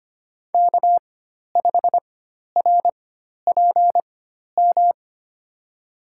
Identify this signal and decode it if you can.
Morse code, heard as a keyed tone, about 25 wpm, spaced as for 12 wpm, 700 Hz. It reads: K5RPM
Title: USA Call Signs - 5 Characters - 4x Character Spacing - 25wpm